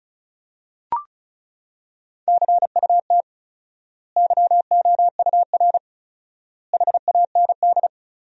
Morse code, heard as a keyed tone, about 34 wpm, 700 Hz. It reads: CUT YOUR HAND
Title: 4x Word-Spacing - Sentences from Top 300 Words - 35wpm